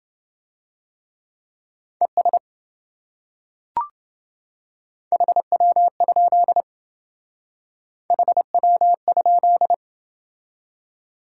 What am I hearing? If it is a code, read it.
Morse code, heard as a keyed tone, about 27 wpm, 700 Hz. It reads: ES HW? HW?